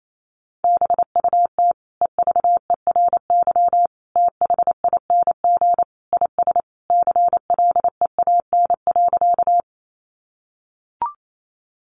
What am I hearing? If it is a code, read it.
Morse code, heard as a keyed tone, about 28 wpm, 700 Hz. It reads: BUT EVERYTHING IS CLEAN.
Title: Sentences from Top 1100 Words 28wpm